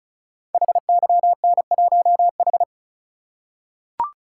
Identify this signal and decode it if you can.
Morse code, heard as a keyed tone, about 36 wpm, 700 Hz. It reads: HYN1H
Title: Code Groups - 5 Letter:Numbers - 35wpm